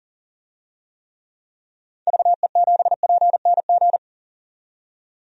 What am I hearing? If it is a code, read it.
Morse code, heard as a keyed tone, about 40 wpm, 700 Hz. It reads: VE7PNG
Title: DXCC-100 Call Signs - 6 Characters - 40wpm